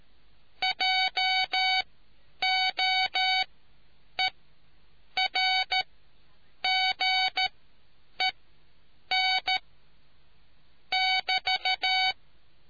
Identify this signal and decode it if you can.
Morse code, heard as a keyed tone, about 13 wpm, spaced as for 8 wpm, 750 Hz. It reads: JOERGEN <BT>